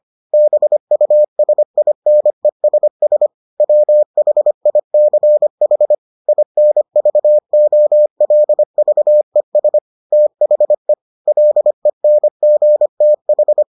Morse code, heard as a keyed tone, about 25 wpm, 600 Hz. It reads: BUSINESS WHICH INVOLVES THE LENGTH